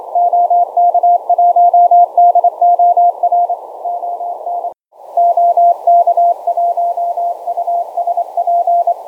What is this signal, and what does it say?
Morse code, heard as a keyed tone, about 26 wpm, 690 Hz. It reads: OK1DOR ON OK1USP